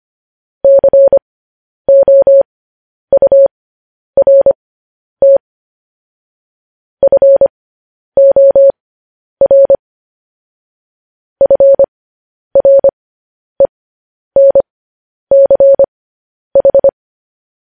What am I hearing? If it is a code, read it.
Morse code, heard as a keyed tone, about 25 wpm, spaced as for 10 wpm, 550 Hz. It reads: COURT FOR FRENCH
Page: news-headlines-in-morse-code-at-10-wpm